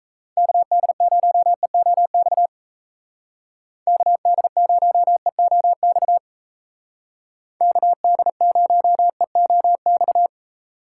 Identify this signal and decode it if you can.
Morse code, heard as a keyed tone, about 33 wpm, 700 Hz. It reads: KD0EOX KD0EOX KD0EOX
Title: Speed Racing - USA Call Signs 28wpm